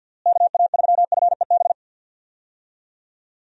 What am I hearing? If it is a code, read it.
Morse code, heard as a keyed tone, about 52 wpm, 700 Hz. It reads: KA3FEB
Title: USA Call Signs - 6 Characters - Repeated 3x - 50wpm